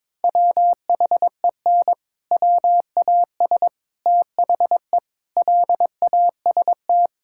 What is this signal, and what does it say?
Morse code, heard as a keyed tone, about 22 wpm, 700 Hz. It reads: WHEN WAS THE LAST
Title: Sentences from Top 1000 Words - Review Part 5 - 22wpm